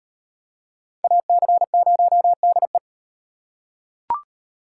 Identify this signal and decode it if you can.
Morse code, heard as a keyed tone, about 38 wpm, 700 Hz. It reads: AC0DE